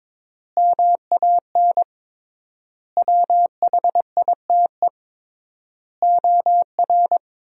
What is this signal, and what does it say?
Morse code, heard as a keyed tone, about 22 wpm, 700 Hz. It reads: MAN WHITE OR
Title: 3x Word-Spacing - Sentences from Top 400 Words - 22wpm